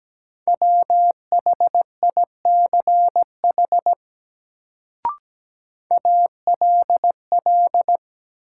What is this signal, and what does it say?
Morse code, heard as a keyed tone, about 17 wpm, 700 Hz. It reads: WHICH ALL
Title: Sets of 2 Words - New Word Drill - 17wpm